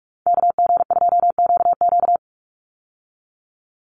Morse code, heard as a keyed tone, about 46 wpm, 690 Hz. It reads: KG2QQ
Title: USA Call Signs 45wpm